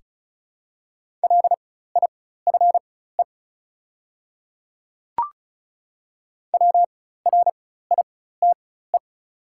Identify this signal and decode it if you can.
Morse code, heard as a keyed tone, about 36 wpm, spaced as for 16 wpm, 690 Hz. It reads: LIFE WRITE
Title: 4x Character Spacing - Top 500 Words - 35wpm